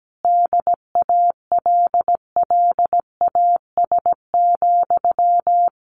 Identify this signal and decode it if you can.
Morse code, heard as a keyed tone, about 17 wpm, 700 Hz. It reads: DALLAS,
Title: Common QSO Phrases Encore 17wpm